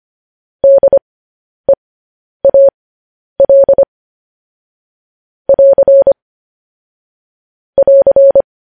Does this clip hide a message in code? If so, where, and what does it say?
Yes: Morse code from 0.6 to 8.4 s, heard as a keyed tone, about 25 wpm, spaced as for 10 wpm, 550 Hz. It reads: DEAL <AR> <AR>